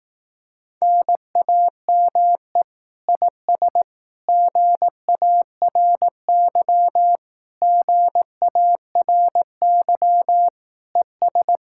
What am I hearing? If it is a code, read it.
Morse code, heard as a keyed tone, about 18 wpm, 700 Hz. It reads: NAME IS GARY GARY ES
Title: Speed-Racing Common QSO Phrases Encore - 15wpm